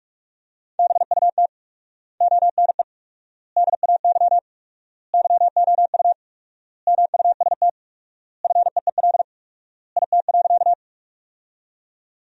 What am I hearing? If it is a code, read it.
Morse code, heard as a keyed tone, about 41 wpm, 700 Hz. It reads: BUT ONE DAY YOU MUST FEEL IT.
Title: 4x Word-Spacing - Sentences from Top 400 Words - 45wpm